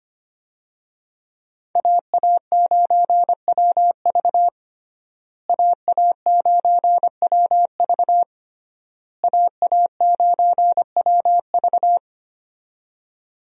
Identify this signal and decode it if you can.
Morse code, heard as a keyed tone, about 25 wpm, 700 Hz. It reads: AA9WV AA9WV AA9WV
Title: USA Call Signs - 5 Characters - Repeated 3x - 25wpm